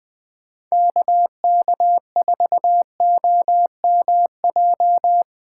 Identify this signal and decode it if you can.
Morse code, heard as a keyed tone, about 20 wpm, 700 Hz. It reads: KK4OMJ
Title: 2023 POTA Hunter Call Signs - 20wpm